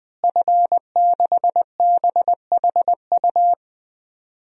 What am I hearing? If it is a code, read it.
Morse code, heard as a keyed tone, about 20 wpm, 700 Hz. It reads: F6BHU
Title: DXCC-100 Call Signs - 5 Characters - Repeated 2x - 20wpm